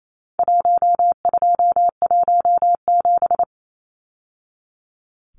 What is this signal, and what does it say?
Morse code, heard as a keyed tone, about 28 wpm, 700 Hz. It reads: 1217
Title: Numbers - Contest Style - 4-Digit 28wpm